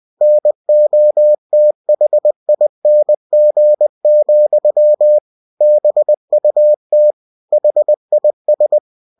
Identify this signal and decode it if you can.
Morse code, heard as a keyed tone, about 20 wpm, 600 Hz. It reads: NOTHING, BUT HIS
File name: cw-books / winnie-the-pooh / winnie-20wpm-page-0004